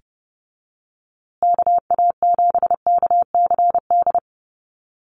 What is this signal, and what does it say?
Morse code, heard as a keyed tone, about 30 wpm, 700 Hz. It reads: KA7KCD
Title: USA Call Signs Repeated 3x 30wpm